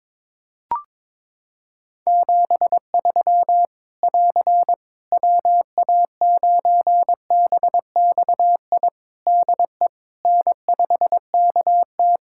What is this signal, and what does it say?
Morse code, heard as a keyed tone, about 22 wpm, 700 Hz. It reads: 73 <AR> WA9BXI DE N5KT
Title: Practice QSO - Exchange 4 - Ending - 22wpm